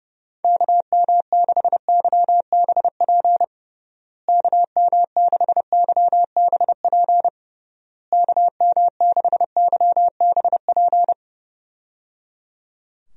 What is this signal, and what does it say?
Morse code, heard as a keyed tone, about 30 wpm, 700 Hz. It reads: KM6YBP KM6YBP KM6YBP
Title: USA Call Signs - 6 Characters - Repeated 3x - 30wpm